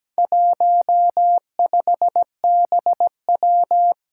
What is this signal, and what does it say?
Morse code, heard as a keyed tone, about 17 wpm, 700 Hz. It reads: 15BW
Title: Four Challenging Letter-Number Combinations 17wpm